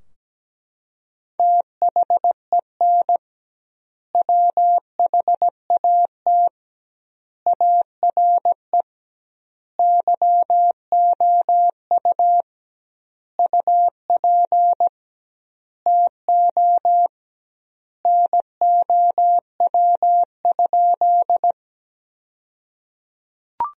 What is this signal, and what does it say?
Morse code, heard as a keyed tone, about 17 wpm, 700 Hz. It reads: THEN WHAT ARE YOU UP TO NOW?
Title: 2x Word-Spacing - Sentences from Top 100 Words - 17wpm